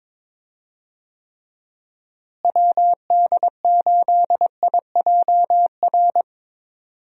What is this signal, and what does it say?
Morse code, heard as a keyed tone, about 21 wpm, 700 Hz. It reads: WD8IJR
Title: USA Call Signs - 6 Characters - Repeated 3x - 22wpm